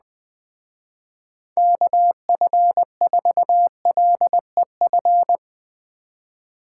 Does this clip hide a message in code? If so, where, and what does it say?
Yes: Morse code from 1.6 to 5.4 s, heard as a keyed tone, about 20 wpm, 700 Hz. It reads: KF4LEF